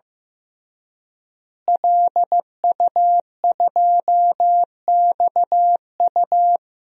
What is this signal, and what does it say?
Morse code, heard as a keyed tone, about 15 wpm, 700 Hz. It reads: LU2XU
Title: DXCC-100 Call Signs - 5 Characters - Repeated 2x - 15wpm